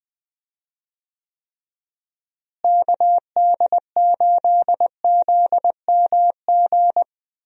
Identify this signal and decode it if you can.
Morse code, heard as a keyed tone, about 20 wpm, 700 Hz. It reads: KD8ZMG
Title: USA Call Signs - 6 Characters - Repeated 2x - 20wpm